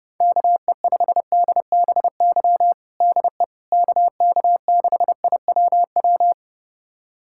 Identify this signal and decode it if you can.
Morse code, heard as a keyed tone, about 30 wpm, 700 Hz. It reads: KE5DBY DE KK6IWW
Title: USA Callsign DE Callsign - 6 Characters - 30wpm